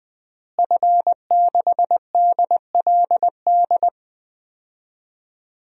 Morse code, heard as a keyed tone, about 21 wpm, 700 Hz. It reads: F6DLD
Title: DXCC-100 Call Signs - 5 Characters - Repeated 3x - 20wpm